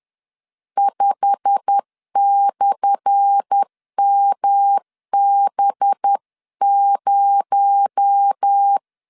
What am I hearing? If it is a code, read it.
Morse code, heard as a keyed tone, about 10 wpm, 800 Hz. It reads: 5/MB0
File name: LEZIONI / 10 SIGNS / 10A SLASH 10wpm 2808